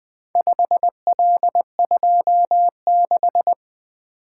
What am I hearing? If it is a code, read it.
Morse code, heard as a keyed tone, about 23 wpm, 700 Hz. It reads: 5L26
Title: Speed-Racing - Four Letter-Number Combinations - 20wpm